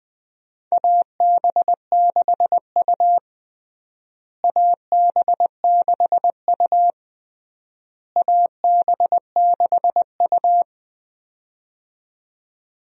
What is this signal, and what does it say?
Morse code, heard as a keyed tone, about 20 wpm, 700 Hz. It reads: AB6U AB6U AB6U
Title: USA Call Signs - 4 Characters - Repeated 3x - 20wpm